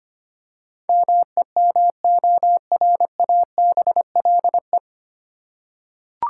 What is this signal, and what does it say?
Morse code, heard as a keyed tone, about 25 wpm, 700 Hz. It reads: MEMORABLE
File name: Nine Letter Words - 25wpm